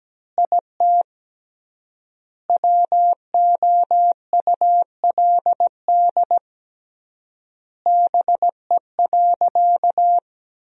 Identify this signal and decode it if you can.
Morse code, heard as a keyed tone, about 17 wpm, 700 Hz. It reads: IT WOULD BE.
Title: 3x Word-Spacing - Sentences from Top 200 Words - 17wpm